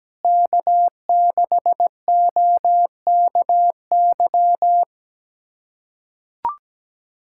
Morse code, heard as a keyed tone, about 17 wpm, 700 Hz. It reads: K6OKY
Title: USA Callsign DE Callsign - 5 Characters - 17wpm